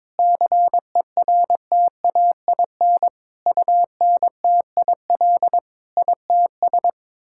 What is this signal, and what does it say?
Morse code, heard as a keyed tone, about 22 wpm, 700 Hz. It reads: CERTAIN UNTIL ITS
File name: practice / 22wpm / Sets of 8 Words - 22wpm